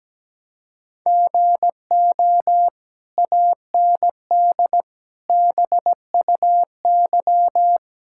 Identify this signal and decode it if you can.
Morse code, heard as a keyed tone, about 17 wpm, 700 Hz. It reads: GO AND BUY
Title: Sentences from Top 1000 Words - Review Part 5 - 17wpm